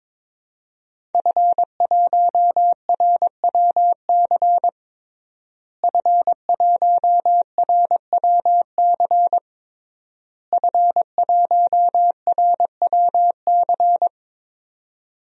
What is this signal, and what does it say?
Morse code, heard as a keyed tone, about 21 wpm, 700 Hz. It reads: F1RWC F1RWC F1RWC
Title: DXCC-100 Call Signs - 5 Characters - Repeated 3x - 22wpm